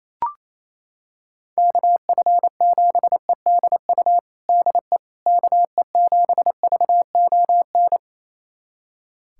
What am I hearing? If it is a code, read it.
Morse code, heard as a keyed tone, about 28 wpm, 700 Hz. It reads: KF7EDU DE KE7VON
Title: USA Callsign DE Callsign - 6 Characters - 28wpm